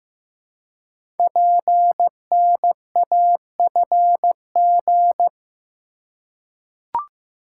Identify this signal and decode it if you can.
Morse code, heard as a keyed tone, about 15 wpm, 700 Hz. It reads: PNAFG
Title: Code Groups - 5 Letters - 15wpm